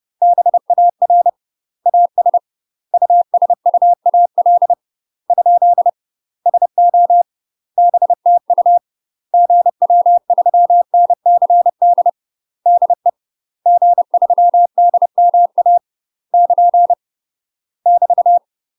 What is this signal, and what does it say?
Morse code, heard as a keyed tone, about 30 wpm, 700 Hz. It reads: BAR AS USUAL ? SO BTU GW3NCD DE G3DMA <KN> <BT>
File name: UK MASTER QSOs TRAINING / 30WPM / UK MASTER QSOs 0018